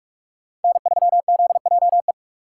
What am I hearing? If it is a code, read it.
Morse code, heard as a keyed tone, about 42 wpm, 700 Hz. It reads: N3ZJE
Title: USA Call Signs - 5 Characters - 3x Character Spacing - 45wpm